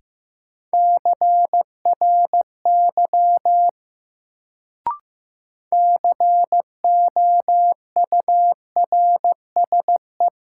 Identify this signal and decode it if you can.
Morse code, heard as a keyed tone, about 15 wpm, 700 Hz. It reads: CRY COURSE
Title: Sentences from Top 500 Words - New Word Drill - 15wpm